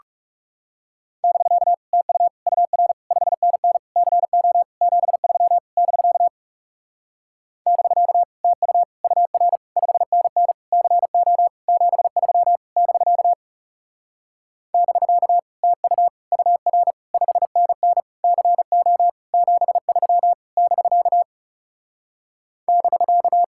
Morse code, heard as a keyed tone, about 37 wpm, 700 Hz. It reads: <BK> TU UR 5NN CO 73 <BK> <BK> TU UR 5NN CO 73 <BK> <BK> TU UR 5NN CO 73 <BK> <BK>
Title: Speed-Racing Common QSO Phrases - 30wpm